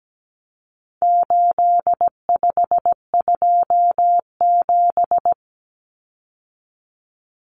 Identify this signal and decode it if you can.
Morse code, heard as a keyed tone, about 17 wpm, 700 Hz. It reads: 8527
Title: Common Cut Numbers - 4-Digit 17wpm